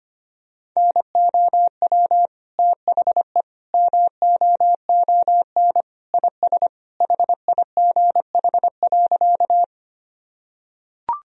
Morse code, heard as a keyed tone, about 25 wpm, 700 Hz. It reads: NOW THE MOON IS HIGH.
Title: Sentences from Top 1000 Words - Review Part 3 - 25wpm